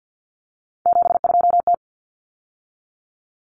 Morse code, heard as a keyed tone, about 50 wpm, 700 Hz. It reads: 72T
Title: Common Cut Numbers - 3-Digit 50wpm